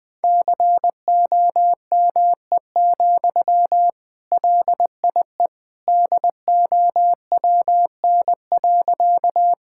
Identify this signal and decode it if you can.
Morse code, heard as a keyed tone, about 20 wpm, 700 Hz. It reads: COME, LIE DOWN.
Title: Sentences from Top 1600 Words 20wpm